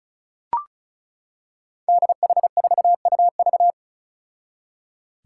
Morse code, heard as a keyed tone, about 35 wpm, 700 Hz. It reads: DH4UV
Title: Five Challenging Letter-Number Combinations 35wpm